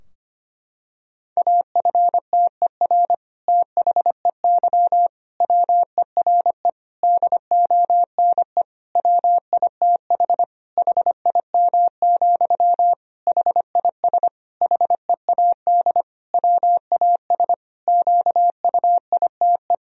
Morse code, heard as a keyed tone, about 25 wpm, 700 Hz. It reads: AFTER THEY WERE DONE WITH HIM, HIS HEAD WAS QUITE
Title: Sentences from Top 1000 Words - Review Part 2 - 25wpm